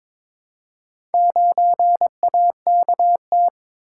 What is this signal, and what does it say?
Morse code, heard as a keyed tone, about 22 wpm, 700 Hz. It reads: 9AKT
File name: practice / 22wpm / Four Letter-Number Combinations - 22wpm